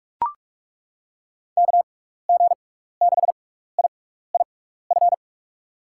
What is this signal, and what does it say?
Morse code, heard as a keyed tone, about 42 wpm, spaced as for 15 wpm, 700 Hz. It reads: KG6IIF
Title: USA Call Signs - 6 Characters - 6x Character Spacing - 45wpm